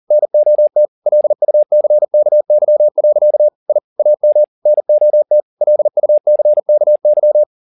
Morse code, heard as a keyed tone, about 40 wpm, 600 Hz. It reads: NOT LUCKY. I AM NOT LUCKY
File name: the old man and the sea 2 40wpm